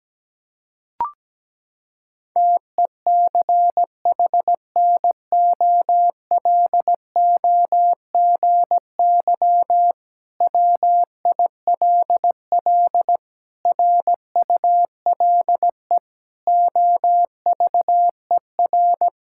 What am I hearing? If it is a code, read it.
Morse code, heard as a keyed tone, about 17 wpm, 700 Hz. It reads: TECHNOLOGY WILL RULE OVER
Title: Sentences from Top 1000 Words - Review Part 5 - 17wpm